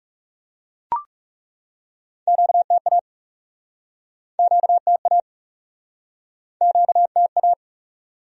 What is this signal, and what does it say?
Morse code, heard as a keyed tone, about 42 wpm, 700 Hz. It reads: QTA QTA QTA